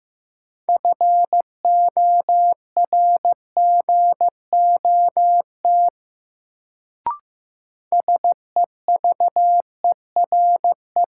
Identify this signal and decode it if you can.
Morse code, heard as a keyed tone, about 15 wpm, 700 Hz. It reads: FORGOT SEVERE
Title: Sentences from Top 1400 Words - New Word Drill - 15wpm